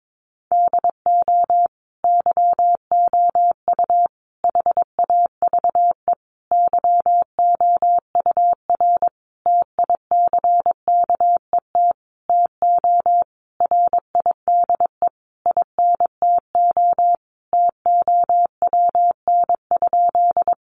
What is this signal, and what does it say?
Morse code, heard as a keyed tone, about 22 wpm, 700 Hz. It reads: DO YOU HAVE YOUR TICKET TO RIDE INTO TOWN?
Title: Sentences from Top 1400 Words 22wpm